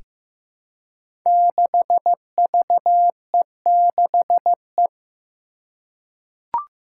Morse code, heard as a keyed tone, about 15 wpm, 700 Hz. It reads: 6VE6E
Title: Five Challenging Letter-Number Combinations 15wpm